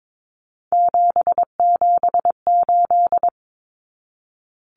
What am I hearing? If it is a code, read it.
Morse code, heard as a keyed tone, about 22 wpm, 700 Hz. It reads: 778
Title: Common Cut Numbers - 3-Digit 22wpm